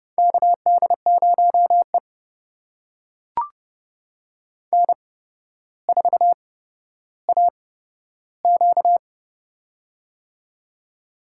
Morse code, heard as a keyed tone, about 30 wpm, spaced as for 13 wpm, 700 Hz. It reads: KD0E N4AQ
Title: USA Call Signs - 4 Characters - 8x Character Spacing - 30wpm